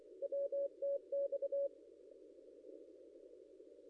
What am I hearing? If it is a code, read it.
Morse code, heard as a keyed tone, about 24 wpm, 560 Hz. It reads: WTX
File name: contests / ss-cw-k5tr-op-n5rz -0800